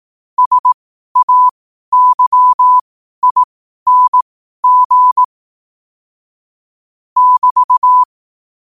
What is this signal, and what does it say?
Morse code, heard as a keyed tone, about 18 wpm, spaced as for 12 wpm, 1000 Hz. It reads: SAYING <BT>
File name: qotdfiles / 12wpm